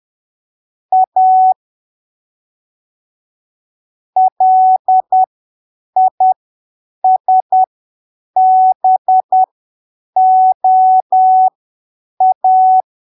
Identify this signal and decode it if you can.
Morse code, heard as a keyed tone, about 10 wpm, spaced as for 7 wpm, 750 Hz. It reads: A LISBOA
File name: arquivos / texto / texto 16 10ppm